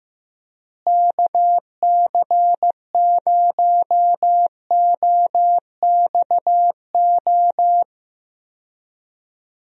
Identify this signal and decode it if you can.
Morse code, heard as a keyed tone, about 15 wpm, 700 Hz. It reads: KC0OXO
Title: USA Call Signs - 6 Characters - Repeated 2x - 15wpm